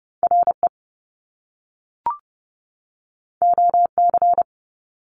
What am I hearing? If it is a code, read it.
Morse code, heard as a keyed tone, about 30 wpm, 700 Hz. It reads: RE OC
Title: Top Two Letter Combinations in Words 30wpm